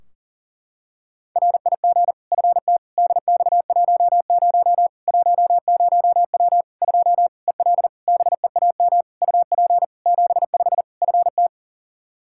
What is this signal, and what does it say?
Morse code, heard as a keyed tone, about 37 wpm, 700 Hz. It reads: RIG FT DX10 10W 2 EL BEAM UP 75 FT
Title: K1USN SST QSO Practice 40wpm